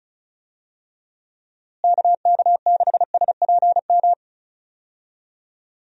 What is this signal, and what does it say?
Morse code, heard as a keyed tone, about 34 wpm, 700 Hz. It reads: KK6SPM
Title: USA Call Signs - 6 Characters - Repeated 2x - 35wpm